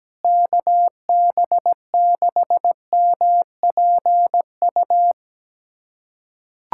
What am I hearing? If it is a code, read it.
Morse code, heard as a keyed tone, about 17 wpm, 700 Hz. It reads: KB6MPU
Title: USA Callsign DE Callsign - 6 Characters - 17wpm